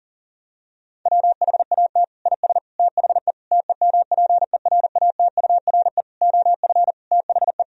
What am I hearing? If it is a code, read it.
Morse code, heard as a keyed tone, about 40 wpm, 700 Hz. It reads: WHAT IS THE TEMPERATURE OF THE
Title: Sentences from Top 1000 Words - Review Part 4 - 40wpm